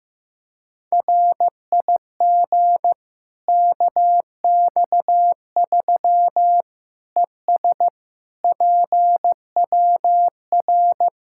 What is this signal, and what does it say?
Morse code, heard as a keyed tone, about 16 wpm, 700 Hz. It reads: RIG KX3 ES PWR
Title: Practice QSO - Exchange 2 - Info - 15wpm